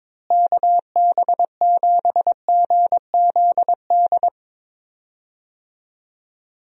Morse code, heard as a keyed tone, about 22 wpm, 700 Hz. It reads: KB7GZD
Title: USA Call Signs - 6 Characters - Repeated 2x - 22wpm